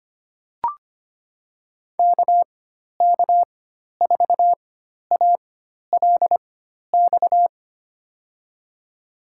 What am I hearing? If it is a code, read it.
Morse code, heard as a keyed tone, about 25 wpm, spaced as for 12 wpm, 700 Hz. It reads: KK4ALX
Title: USA Call Signs - 6 Characters - 4x Character Spacing - 25wpm